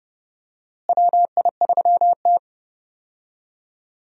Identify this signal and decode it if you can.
Morse code, heard as a keyed tone, about 31 wpm, 700 Hz. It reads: WI3T
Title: USA Call Signs - 4 Characters - Repeated 3x - 30wpm